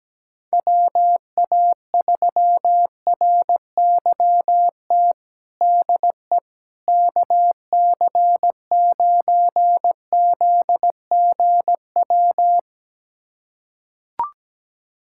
Morse code, heard as a keyed tone, about 17 wpm, 700 Hz. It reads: WA3RYT DE KC9ZGW
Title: USA Callsign DE Callsign 17wpm